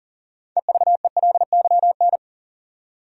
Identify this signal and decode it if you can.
Morse code, heard as a keyed tone, about 40 wpm, 700 Hz. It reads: EVELYN
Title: USA Common Names - 40wpm